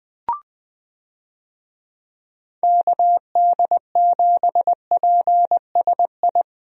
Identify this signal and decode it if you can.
Morse code, heard as a keyed tone, about 21 wpm, 700 Hz. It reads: KD7PSI
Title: USA Call Signs - 6 Characters - 20wpm